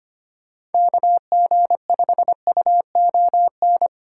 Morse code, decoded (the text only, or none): KG5UON